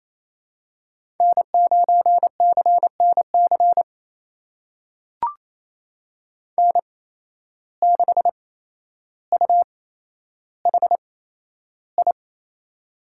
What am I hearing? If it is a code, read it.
Morse code, heard as a keyed tone, about 27 wpm, spaced as for 12 wpm, 700 Hz. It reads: N9CNC N6UHI
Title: USA Call Signs - 5 Characters - 8x Character Spacing - 28wpm